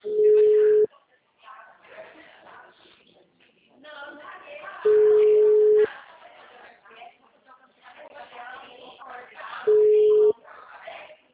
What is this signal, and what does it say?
Ringback tone (Europe, 425 Hz), with speech.